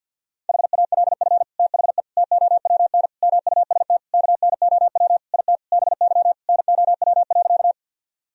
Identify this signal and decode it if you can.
Morse code, heard as a keyed tone, about 52 wpm, 700 Hz. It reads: HALF THE TOWN MUST KNOW IT BY NOW.
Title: Sentences from Top 1000 Words - Review Part 2 - 50wpm